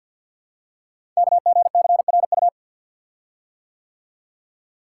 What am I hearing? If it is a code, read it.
Morse code, heard as a keyed tone, about 48 wpm, 700 Hz. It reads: KGCRU
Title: Code Groups - 5 Letters - 50wpm